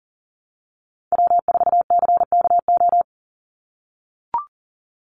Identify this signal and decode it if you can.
Morse code, heard as a keyed tone, about 39 wpm, 700 Hz. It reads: W4CKO
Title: USA Call Signs 40wpm